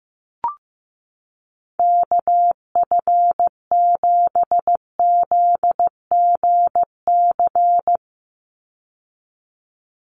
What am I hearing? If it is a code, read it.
Morse code, heard as a keyed tone, about 15 wpm, 700 Hz. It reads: KF7ZGC
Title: USA Call Signs 15wpm